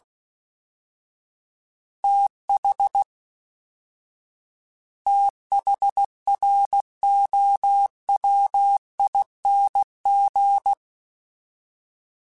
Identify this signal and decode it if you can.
Morse code, heard as a keyed tone, about 16 wpm, 800 Hz. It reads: TH THROWING
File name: TH / TH mp3s / TH Prefixes 3x2 @13